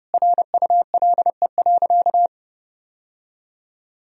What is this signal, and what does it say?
Morse code, heard as a keyed tone, about 30 wpm, 700 Hz. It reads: RULE.
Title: Sentences from Top 1000 Words - Review Part 2 - 30wpm